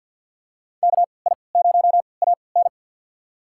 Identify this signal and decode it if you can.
Morse code, heard as a keyed tone, about 52 wpm, spaced as for 29 wpm, 700 Hz. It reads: KI0AN
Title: USA Call Signs - 5 Characters - 3x Character Spacing - 50wpm